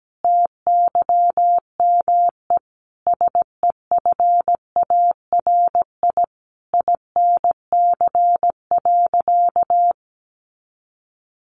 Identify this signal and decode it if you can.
Morse code, heard as a keyed tone, about 17 wpm, 700 Hz. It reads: TYME SEFARI INC.
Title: A Sound Of Thunder 17wpm